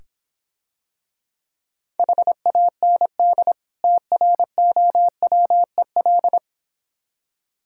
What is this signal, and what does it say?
Morse code, heard as a keyed tone, about 26 wpm, 690 Hz. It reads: HAND TROWEL